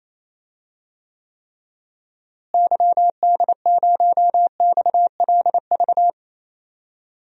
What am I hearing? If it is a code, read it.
Morse code, heard as a keyed tone, about 28 wpm, 700 Hz. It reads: YD0XLV
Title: DXCC-100 Call Signs - 6 Characters - Repeated 3x - 28wpm